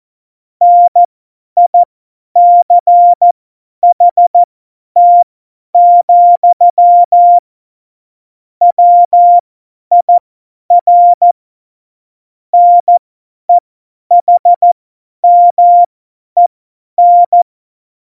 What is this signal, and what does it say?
Morse code, heard as a keyed tone, about 14 wpm, spaced as for 10 wpm, 700 Hz. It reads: NICHT, WIR NEHMEN
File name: Morsekurs / Witze / Witz 28. November 2017, 50 F70